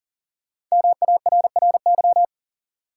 Morse code, heard as a keyed tone, about 40 wpm, 700 Hz. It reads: MARRY
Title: Sentences from Top 1200 Words - New Word Drill - 40wpm